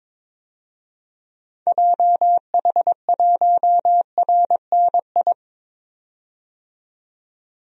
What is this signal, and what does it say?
Morse code, heard as a keyed tone, about 22 wpm, 700 Hz. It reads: JH1RNI
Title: DXCC-100 Call Signs - 6 Characters - 22wpm